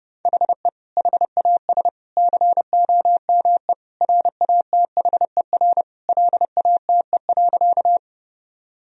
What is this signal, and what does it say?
Morse code, heard as a keyed tone, about 30 wpm, 700 Hz. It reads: HE HAS COME RATHER LATE.
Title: Sentences from Top 1000 Words - Review Part 4 - 30wpm